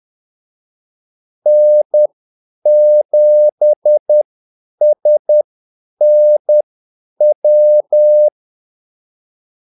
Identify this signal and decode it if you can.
Morse code, heard as a keyed tone, about 10 wpm, spaced as for 8 wpm, 600 Hz. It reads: N7SNW
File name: Callsigns-10-8wpm